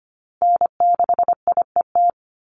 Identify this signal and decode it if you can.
Morse code, heard as a keyed tone, about 25 wpm, 700 Hz. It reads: N6IET
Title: Call Signs Repeated 2x - 25wpm